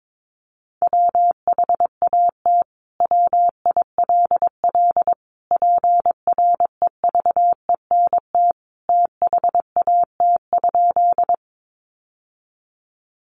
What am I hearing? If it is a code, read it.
Morse code, heard as a keyed tone, about 22 wpm, 700 Hz. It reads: WHAT WILL PREVENT THAT?
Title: Sentences from Top 1100 Words 22wpm